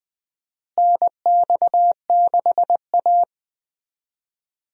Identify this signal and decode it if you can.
Morse code, heard as a keyed tone, about 20 wpm, 700 Hz. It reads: NX6A
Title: USA Call Signs - 4 Characters - Repeated 3x - 20wpm